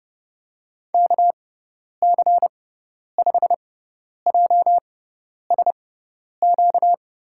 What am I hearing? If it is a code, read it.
Morse code, heard as a keyed tone, about 32 wpm, spaced as for 10 wpm, 700 Hz. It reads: KC5JSQ
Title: USA Call Signs - 6 Characters - 6x Character Spacing - 30wpm